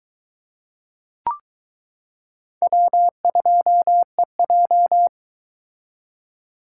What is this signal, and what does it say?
Morse code, heard as a keyed tone, about 22 wpm, 700 Hz. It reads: W2EJ